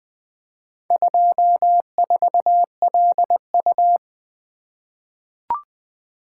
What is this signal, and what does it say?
Morse code, heard as a keyed tone, about 20 wpm, 700 Hz. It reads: 24LU
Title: Four Letter-Number Combinations - 20wpm